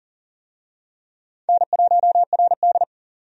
Morse code, heard as a keyed tone, about 40 wpm, 700 Hz. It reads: N1RD
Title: USA Call Signs - 4 Characters - 40wpm